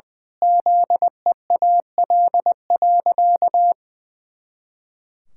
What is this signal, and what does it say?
Morse code, heard as a keyed tone, about 20 wpm, 700 Hz. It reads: ZEAL.
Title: Pangram Challenge v2 - 20wpm